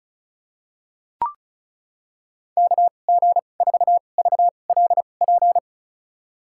Morse code, heard as a keyed tone, about 36 wpm, spaced as for 25 wpm, 700 Hz. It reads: KG4VLP
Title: USA Call Signs - 6 Characters - 2x Character Spacing - 35wpm